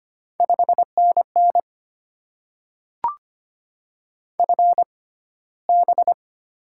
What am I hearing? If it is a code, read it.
Morse code, heard as a keyed tone, about 25 wpm, spaced as for 15 wpm, 700 Hz. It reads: 5NN FB
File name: Top 20 QSO Elements - 6x Character Spacing - 25wpm